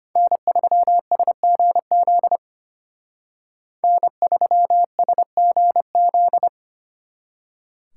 Morse code, heard as a keyed tone, about 31 wpm, 700 Hz. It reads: N3SGZ N3SGZ
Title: Speed Racing - USA 1x3 Call Signs 25wpm